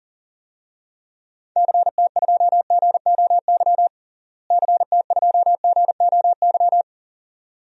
Morse code, heard as a keyed tone, about 40 wpm, 700 Hz. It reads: CT2GOY CT2GOY
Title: DXCC-100 Call Signs - 6 Characters - Repeated 2x - 40wpm